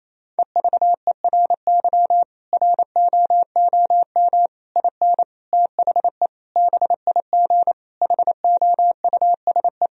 Morse code, heard as a keyed tone, about 28 wpm, 700 Hz. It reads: EVERY ROOM IN THE BIG HOUSE
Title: Sentences from Top 1000 Words - Review Part 3 - 28wpm